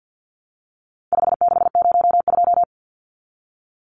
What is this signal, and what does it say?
Morse code, heard as a keyed tone, about 50 wpm, 700 Hz. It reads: 5602
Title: Numbers - 4-Digit 50wpm